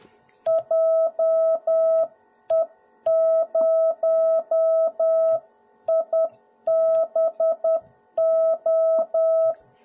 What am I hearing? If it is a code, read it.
Morse code, heard as a keyed tone, about 10 wpm, 650 Hz. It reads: JE0IBO
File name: jh8cbh / 2021cw / cw01b